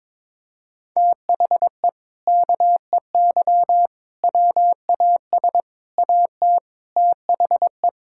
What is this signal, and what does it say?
Morse code, heard as a keyed tone, about 22 wpm, 700 Hz. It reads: THE KEY WAS AT THE
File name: practice / 22wpm / Sentences with 4-Word Phrases - 22wpm